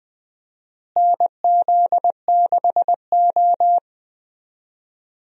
Morse code, heard as a keyed tone, about 20 wpm, 700 Hz. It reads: NZ6O
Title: USA Call Signs - 4 Characters - Repeated 3x - 20wpm